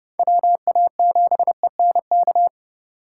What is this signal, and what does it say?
Morse code, heard as a keyed tone, about 30 wpm, 700 Hz. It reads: WA7ENK